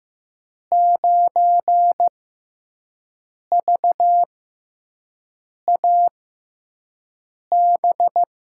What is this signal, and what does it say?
Morse code, heard as a keyed tone, about 15 wpm, spaced as for 5 wpm, 700 Hz. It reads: 9VAB
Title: USA Call Signs - 5 Characters - 6x Character Spacing - 15wpm